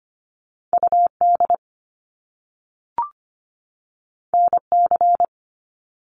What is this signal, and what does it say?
Morse code, heard as a keyed tone, about 25 wpm, 700 Hz. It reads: UD NC
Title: Top Two Letter Combinations in Words 25wpm